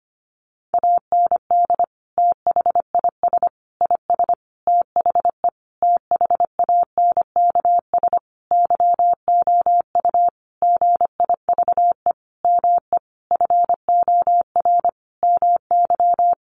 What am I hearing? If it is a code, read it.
Morse code, heard as a keyed tone, about 25 wpm, 700 Hz. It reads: AND THIS IS THE THANKS YOU GIVE ME FOR MY
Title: Sentences from Top 1200 Words 25wpm